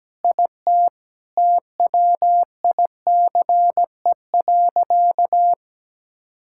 Morse code, heard as a keyed tone, about 17 wpm, 700 Hz. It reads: IT TWICE.
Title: Sentences from Top 1600 Words 17wpm